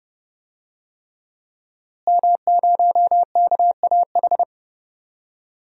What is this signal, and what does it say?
Morse code, heard as a keyed tone, about 30 wpm, 700 Hz. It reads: M0KAH
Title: DXCC-100 Call Signs - 5 Characters - Repeated 3x - 30wpm